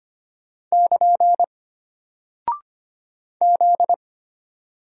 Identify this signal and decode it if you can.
Morse code, heard as a keyed tone, about 25 wpm, 700 Hz. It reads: <KN> Z
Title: Single Character (Letter:Number:Punctuation:Prosign) Alt 25wpm